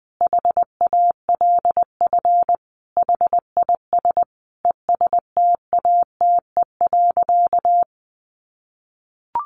Morse code, heard as a keyed tone, about 19 wpm, 700 Hz. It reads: HALF HIS ESTATE.
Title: Sentences from Top 1200 Words 20wpm